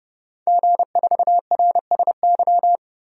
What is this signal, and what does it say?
Morse code, heard as a keyed tone, about 32 wpm, 700 Hz. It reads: G4RSY
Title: DXCC-100 Call Signs - 5 Characters - 30wpm